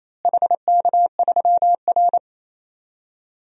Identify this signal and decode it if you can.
Morse code, heard as a keyed tone, about 27 wpm, 700 Hz. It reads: HK3R
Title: DXCC-100 Call Signs - 4 Characters - Repeated 3x - 28wpm